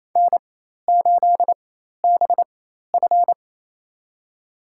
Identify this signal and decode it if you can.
Morse code, heard as a keyed tone, about 28 wpm, spaced as for 13 wpm, 700 Hz. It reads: N8BF
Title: USA Call Signs - 4x Character Spacing - 28wpm